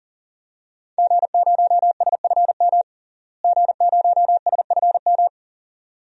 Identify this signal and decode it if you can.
Morse code, heard as a keyed tone, about 40 wpm, 700 Hz. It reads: G0SFM G0SFM
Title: DXCC-100 Call Signs - 5 Characters - Repeated 2x - 40wpm